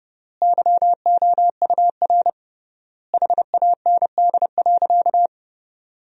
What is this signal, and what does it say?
Morse code, heard as a keyed tone, about 30 wpm, 700 Hz. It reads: YOUR HAND.
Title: 3x Word-Spacing - Sentences from Top 400 Words - 30wpm